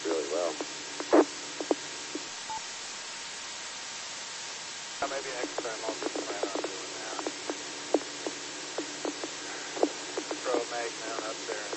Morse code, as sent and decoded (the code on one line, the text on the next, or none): none